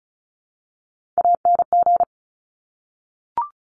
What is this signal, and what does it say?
Morse code, heard as a keyed tone, about 36 wpm, 700 Hz. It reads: ANG
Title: Top Three Letter Combinations in Words 35wpm